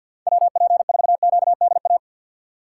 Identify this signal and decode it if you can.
Morse code, heard as a keyed tone, about 50 wpm, 700 Hz. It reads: WP4QDA
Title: USA Call Signs - 6 Characters - 3x Character Spacing - 50wpm